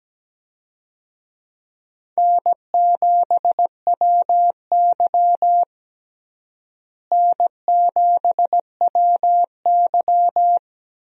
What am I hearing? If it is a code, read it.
Morse code, heard as a keyed tone, about 17 wpm, 700 Hz. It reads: N7WY N7WY
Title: USA Call Signs - 4 Characters - Repeated 3x - 17wpm